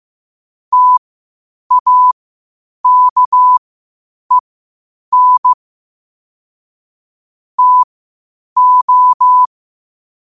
Morse code, heard as a keyed tone, about 15 wpm, spaced as for 8 wpm, 1000 Hz. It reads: TAKEN TO
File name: qotdfiles / 8wpm